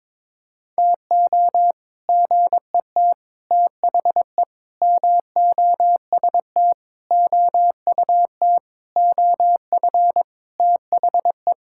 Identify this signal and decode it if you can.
Morse code, heard as a keyed tone, about 22 wpm, 700 Hz. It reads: TO GET THE MOST OUT OF THE
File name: Sentences with 7-Word Phrases - 22wpm